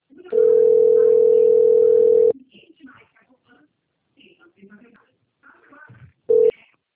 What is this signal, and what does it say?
Ringback tone (North America), 440 + 480 Hz, with speech.
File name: rg-1600-unknown- - - .12908